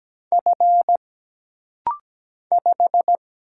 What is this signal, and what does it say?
Morse code, heard as a keyed tone, about 17 wpm, 700 Hz. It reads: F 5
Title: Single Letter-Number Alt - Rapid-Fire 17wpm